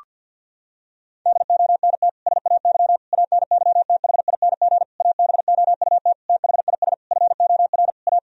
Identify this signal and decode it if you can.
Morse code, heard as a keyed tone, about 50 wpm, 700 Hz. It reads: DONT SAY ANYTHING ABOUT THIS FOR A
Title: Sentences from Top 1000 Words - Review Part 1 - 50wpm